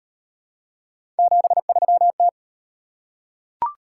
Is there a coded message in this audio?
Yes: Morse code from 1.2 to 2.3 s, heard as a keyed tone, about 35 wpm, 700 Hz. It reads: 73T